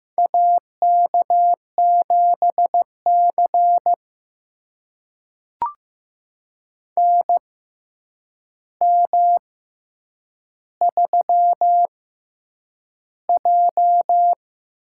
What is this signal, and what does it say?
Morse code, heard as a keyed tone, about 15 wpm, spaced as for 8 wpm, 700 Hz. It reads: AK7C NM3J
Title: USA Call Signs - 6x Character Spacing - 15wpm